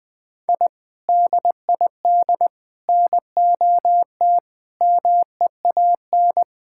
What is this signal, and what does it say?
Morse code, heard as a keyed tone, about 20 wpm, 700 Hz. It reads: I DID NOT MEAN
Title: Sentences with 5-Word Phrases - 20wpm